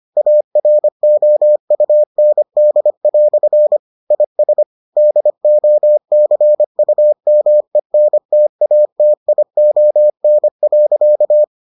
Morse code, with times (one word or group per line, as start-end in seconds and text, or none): 0.2-3.8 AROUND"
4.1-4.6 IS
5.0-11.4 DOCUMENTATION.